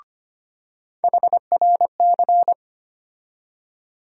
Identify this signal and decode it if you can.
Morse code, heard as a keyed tone, about 25 wpm, 700 Hz. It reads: HRC
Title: USA Call Sign Postfixes - 3 Letters 25wpm